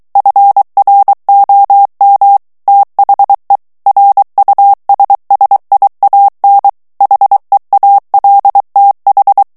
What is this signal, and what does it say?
Morse code, heard as a keyed tone, about 23 wpm, 800 Hz. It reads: FROM THE RUSSIAN HEALTH
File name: morse mp3 / ANGLAIS texte clair 2 vitesse 22 mots minute